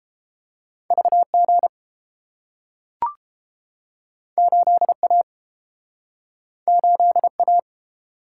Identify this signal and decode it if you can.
Morse code, heard as a keyed tone, about 30 wpm, 700 Hz. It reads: VG 8A 8A